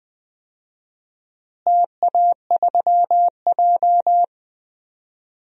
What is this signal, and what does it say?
Morse code, heard as a keyed tone, about 20 wpm, 700 Hz. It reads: TA3J
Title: DXCC-100 Call Signs - 4 Characters - Repeated 3x - 20wpm